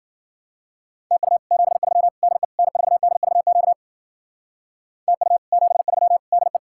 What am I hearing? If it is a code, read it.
Morse code, heard as a keyed tone, about 52 wpm, 700 Hz. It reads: TU 73 DE N4DVQ TU 73 DE